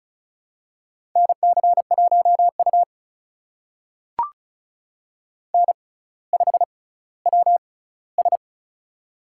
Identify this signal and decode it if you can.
Morse code, heard as a keyed tone, about 36 wpm, spaced as for 18 wpm, 700 Hz. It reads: NC1U N5WS
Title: USA Call Signs - 6x Character Spacing - 35wpm